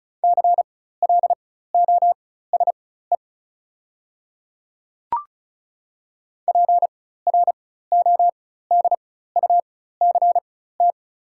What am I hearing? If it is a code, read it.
Morse code, heard as a keyed tone, about 36 wpm, spaced as for 16 wpm, 700 Hz. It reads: CLOSE PRODUCT
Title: 4x Character Spacing - Top 500 Words - 35wpm